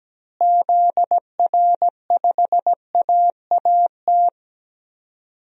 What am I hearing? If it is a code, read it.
Morse code, heard as a keyed tone, about 17 wpm, 700 Hz. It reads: ZR5AAT
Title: DXCC-100 Call Signs - 6 Characters - Repeated 2x - 17wpm